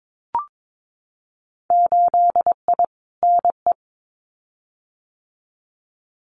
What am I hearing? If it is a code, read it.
Morse code, heard as a keyed tone, about 22 wpm, 700 Hz. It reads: 8I NE
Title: Winter Field Day Exchanges Encore 22wpm